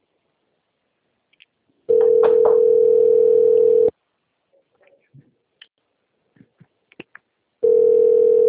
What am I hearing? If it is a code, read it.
Ringback tone (North America), 440 + 480 Hz.